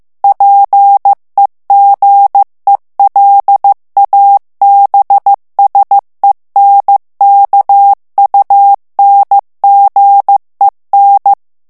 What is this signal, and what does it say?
Morse code, heard as a keyed tone, about 15 wpm, 800 Hz. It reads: PEGELABSENKUNGEN
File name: morse mp3 / ALLEMAND texte clair 1 vitesse 14 mots minute